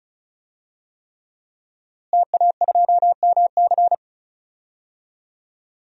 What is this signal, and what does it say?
Morse code, heard as a keyed tone, about 34 wpm, 700 Hz. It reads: TA2MC
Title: DXCC-100 Call Signs - 5 Characters - Repeated 3x - 35wpm